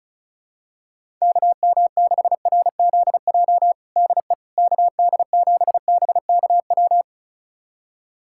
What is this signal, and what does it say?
Morse code, heard as a keyed tone, about 34 wpm, 700 Hz. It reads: KM6RZJ DE KD7BKW
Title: USA Callsign DE Callsign - 6 Characters - 35wpm